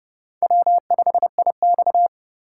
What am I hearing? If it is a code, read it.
Morse code, heard as a keyed tone, about 37 wpm, 700 Hz. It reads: W5IX
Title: Speed Racing - USA 1x2 Call Signs 30wpm